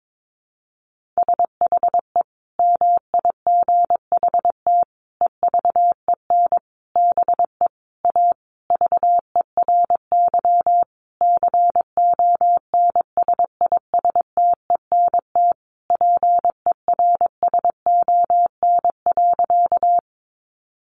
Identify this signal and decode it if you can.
Morse code, heard as a keyed tone, about 22 wpm, 700 Hz. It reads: SHE MIGHT EVEN BE A VERY CONSISTENT PERSON.
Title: Sentences from Top 1500 Words 22wpm